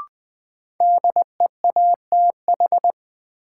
Morse code, heard as a keyed tone, about 20 wpm, 700 Hz. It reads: DEATH
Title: Sentences from Top 1100 Words - New Word Drill Alt - 20wpm